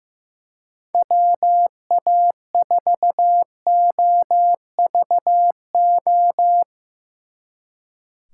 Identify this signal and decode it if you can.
Morse code, heard as a keyed tone, about 16 wpm, 700 Hz. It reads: WA4OVO
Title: Speed Racing - USA 2x3 Call Signs 15wpm